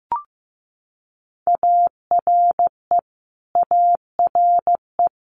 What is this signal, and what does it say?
Morse code, heard as a keyed tone, about 15 wpm, 700 Hz. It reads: ARE ARE
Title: Multi-letter Phonemes Repeated 3x 15wpm